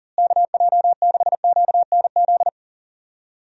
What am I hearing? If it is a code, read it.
Morse code, heard as a keyed tone, about 40 wpm, 700 Hz. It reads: KJ6QNZ
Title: USA Call Signs - 6 Characters - Repeated 2x - 40wpm